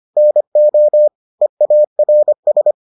Morse code, heard as a keyed tone, about 25 wpm, 600 Hz. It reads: NO EARS